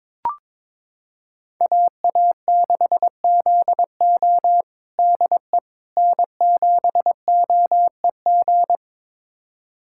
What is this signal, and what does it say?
Morse code, heard as a keyed tone, about 22 wpm, 700 Hz. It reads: AA6ZO DE N7OEG
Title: USA Callsign DE Callsign - 5 Characters - 22wpm